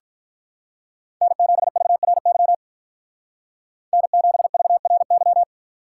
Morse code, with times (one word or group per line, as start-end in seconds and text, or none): 1.2-2.5 N7VRY
3.9-5.4 N7VRY